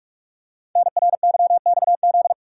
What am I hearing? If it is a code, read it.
Morse code, heard as a keyed tone, about 44 wpm, 690 Hz. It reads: NRYXZ
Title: Code Groups - 5 Letters - 45wpm